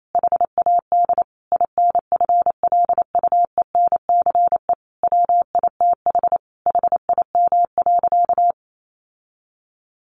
Morse code, heard as a keyed tone, about 28 wpm, 700 Hz. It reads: HAD INFLUENCE WITH HIM.
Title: Sentences from Top 1100 Words 28wpm